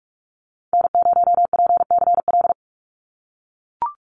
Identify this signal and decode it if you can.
Morse code, heard as a keyed tone, about 44 wpm, 700 Hz. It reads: N0PCL
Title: NA SOTA Activator Callsigns 45wpm